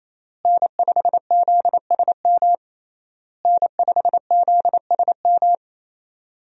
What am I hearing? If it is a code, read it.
Morse code, heard as a keyed tone, about 28 wpm, 700 Hz. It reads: N5ZSM N5ZSM
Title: USA Call Signs - 5 Characters - Repeated 3x - 28wpm